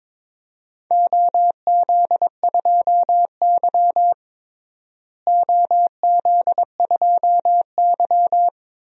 Morse code, heard as a keyed tone, about 22 wpm, 700 Hz. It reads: OZ2Y OZ2Y
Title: DXCC-100 Call Signs - 4 Characters - Repeated 2x - 22wpm